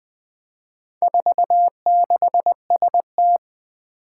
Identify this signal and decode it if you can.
Morse code, heard as a keyed tone, about 20 wpm, 700 Hz. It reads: 46ST
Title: Four Challenging Letter-Number Combinations 20wpm